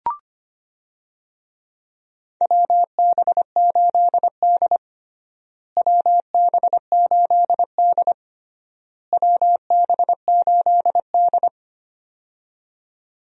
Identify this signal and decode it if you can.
Morse code, heard as a keyed tone, about 25 wpm, 700 Hz. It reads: WB8D WB8D WB8D
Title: USA Call Signs - 4 Characters - Repeated 3x - 25wpm